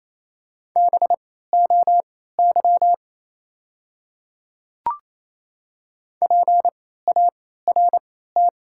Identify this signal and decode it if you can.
Morse code, heard as a keyed tone, about 28 wpm, spaced as for 16 wpm, 700 Hz. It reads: BOY PART
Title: 3x Character Spacing - Top 500 Words - 28wpm